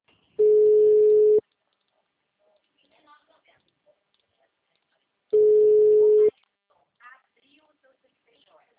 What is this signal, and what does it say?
Ringback tone (Europe, 425 Hz), with speech.